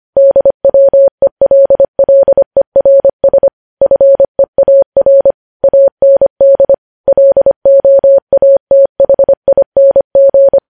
Morse code, heard as a keyed tone, about 25 wpm, 550 Hz. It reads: DWELLERS FEAR AND LOATHING
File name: podcast / News headlines-1x-25x25